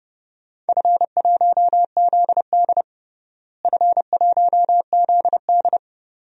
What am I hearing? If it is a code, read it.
Morse code, heard as a keyed tone, about 30 wpm, 700 Hz. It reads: F1ZD F1ZD
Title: DXCC-100 Call Signs - 4 Characters - Repeated 2x - 30wpm